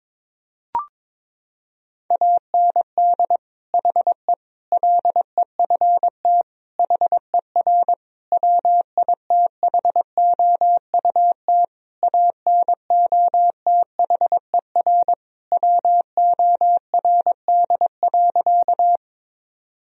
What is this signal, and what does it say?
Morse code, heard as a keyed tone, about 22 wpm, 700 Hz. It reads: AND HE LEFT HER WITHOUT ANOTHER WORD.
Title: Sentences from Top 1000 Words - Review Part 3 - 22wpm